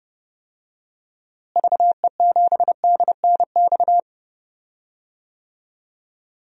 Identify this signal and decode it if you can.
Morse code, heard as a keyed tone, about 30 wpm, 700 Hz. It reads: VE7DNX
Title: DXCC-100 Call Signs - 6 Characters - Repeated 2x - 30wpm